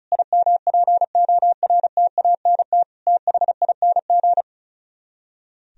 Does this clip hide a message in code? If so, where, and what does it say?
Yes: Morse code from 0.1 to 4.4 s, heard as a keyed tone, about 44 wpm, 700 Hz. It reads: IMPORTANT THING